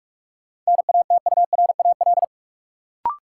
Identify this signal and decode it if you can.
Morse code, heard as a keyed tone, about 44 wpm, 700 Hz. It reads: NATURAL
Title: Sentences from Top 700 Words - New Word Drill - 45wpm